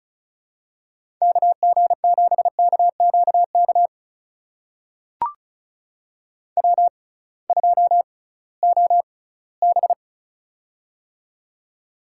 Morse code, heard as a keyed tone, about 34 wpm, spaced as for 20 wpm, 700 Hz. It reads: KG7KQK W2OB
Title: USA Call Signs - 6x Character Spacing - 35wpm